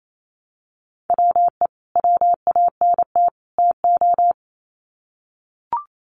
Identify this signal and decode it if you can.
Morse code, heard as a keyed tone, about 28 wpm, 700 Hz. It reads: WE WANT TO
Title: Sets of 3 Words 28wpm